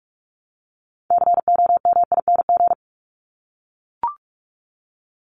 Morse code, heard as a keyed tone, about 44 wpm, 700 Hz. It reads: COMING
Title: ING Suffix - Top 100 Words with Base Words 45wpm